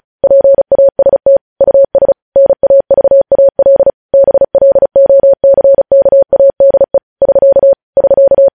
Morse code, heard as a keyed tone, about 36 wpm, 550 Hz. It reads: PAST US NAVAL BLOCKADE <SK> <SK>
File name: podcast / News headlines-1x-35x35